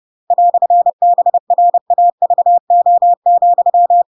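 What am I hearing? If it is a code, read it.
Morse code, heard as a keyed tone, about 30 wpm, 700 Hz. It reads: "BRAVO,
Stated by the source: Chapter 2 of The Adventures of Pinocchio read as a book in Morse code (CW) at 30 wpm, by ik1hge.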